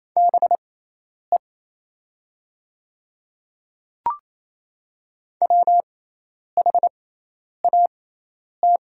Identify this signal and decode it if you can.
Morse code, heard as a keyed tone, about 28 wpm, spaced as for 10 wpm, 700 Hz. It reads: BE WHAT
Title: 6x Character Spacing - Top 200 Words - 28wpm